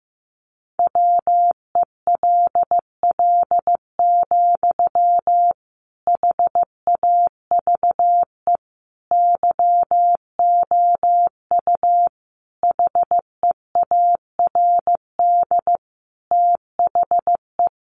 Morse code, heard as a keyed tone, about 15 wpm, 700 Hz. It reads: WELL, HAVE YOU HEARD THE
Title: Sentences from Top 800 Words 15wpm